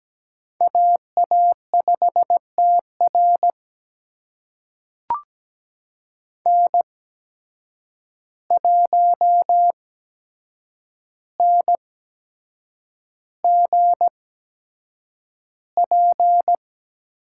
Morse code, heard as a keyed tone, about 17 wpm, spaced as for 7 wpm, 700 Hz. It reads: AA5TR N1NGP
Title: USA Call Signs - 8x Character Spacing - 17wpm